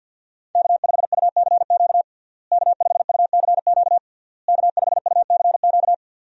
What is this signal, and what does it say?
Morse code, heard as a keyed tone, about 51 wpm, 700 Hz. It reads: K5UCQ K5UCQ K5UCQ
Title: USA Call Signs - 5 Characters - Repeated 3x - 50wpm